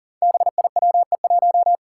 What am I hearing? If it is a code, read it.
Morse code, heard as a keyed tone, about 40 wpm, 700 Hz. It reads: BIWE1
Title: Five Challenging Letter-Number Combinations 40wpm